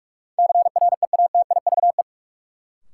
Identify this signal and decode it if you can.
Morse code, heard as a keyed tone, about 46 wpm, 700 Hz. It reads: CREATIVE
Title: Sentences from Top 1000 Words - Review Part 5 - New Word Drill - 45wpm